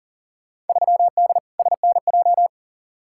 Morse code, heard as a keyed tone, about 40 wpm, 700 Hz. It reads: 3D SNJ
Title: Field Day Exchanges Encore - 40wpm